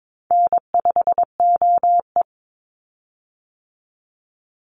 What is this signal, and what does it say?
Morse code, heard as a keyed tone, about 22 wpm, 700 Hz. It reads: N5OE
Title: Call Signs - 22wpm